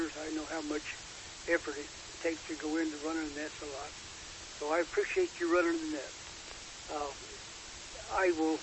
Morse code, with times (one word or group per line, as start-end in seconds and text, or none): none